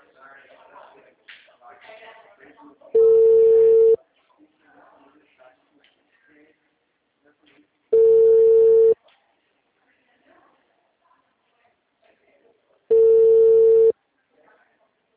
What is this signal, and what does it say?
Ringback tone (China, 450 Hz), with speech.